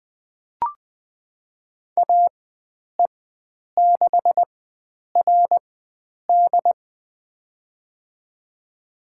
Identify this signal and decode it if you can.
Morse code, heard as a keyed tone, about 20 wpm, spaced as for 9 wpm, 700 Hz. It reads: AE6RD
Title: USA Call Signs - 5 Characters - 4x Character Spacing - 20wpm